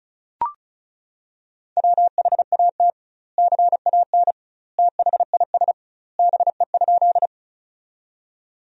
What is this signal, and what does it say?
Morse code, heard as a keyed tone, about 34 wpm, 700 Hz. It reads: WHAT CAN THIS BE?
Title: 2x Word-Spacing - Sentences from Top 100 Words - 35wpm